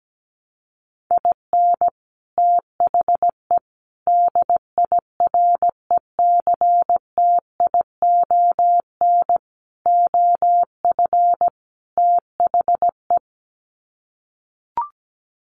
Morse code, heard as a keyed tone, about 17 wpm, 700 Hz. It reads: IN THE DIRECTION OF THE
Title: Sets of 5 Words 17wpm